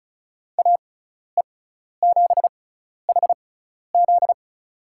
Morse code, heard as a keyed tone, about 36 wpm, spaced as for 12 wpm, 700 Hz. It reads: AE7HZ
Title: USA Call Signs - 5 Characters - 6x Character Spacing - 35wpm